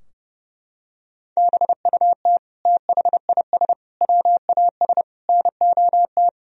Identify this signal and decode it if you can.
Morse code, heard as a keyed tone, about 30 wpm, 700 Hz. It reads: BUT THIS WAS NOT
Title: Sentences from Top 1000 Words - Review Part 3 - 30wpm